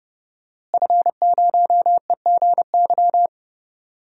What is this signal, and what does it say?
Morse code, heard as a keyed tone, about 30 wpm, 700 Hz. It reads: F0EGY
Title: DXCC-100 Call Signs - 5 Characters - Repeated 2x - 30wpm